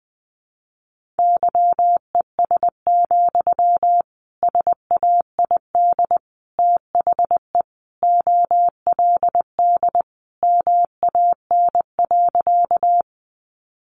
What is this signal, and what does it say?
Morse code, heard as a keyed tone, about 20 wpm, 700 Hz. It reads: YES, SAID THE OLD MAN.
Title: Sentences from Top 500 Words 20wpm